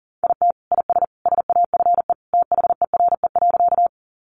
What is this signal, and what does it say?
Morse code, heard as a keyed tone, about 40 wpm, 700 Hz. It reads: IT IS SAFE THERE.
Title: Easy Phrases 40wpm